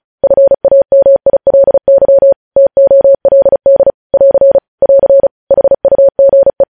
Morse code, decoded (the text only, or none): FAMILY TOLD <AR> <AR> HUGE